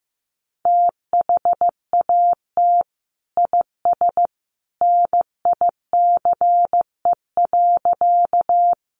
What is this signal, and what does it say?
Morse code, heard as a keyed tone, about 15 wpm, 700 Hz. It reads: THAT IS NICE.
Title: Sentences from Top 1100 Words 15wpm